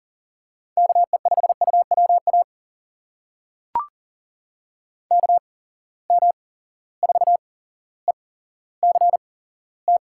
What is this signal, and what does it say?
Morse code, heard as a keyed tone, about 40 wpm, spaced as for 17 wpm, 700 Hz. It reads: KE5UWA KM4ECT
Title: USA Call Signs - 6 Characters - 8x Character Spacing - 40wpm